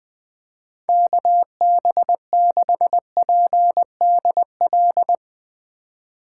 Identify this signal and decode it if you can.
Morse code, heard as a keyed tone, about 20 wpm, 700 Hz. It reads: KB6PDL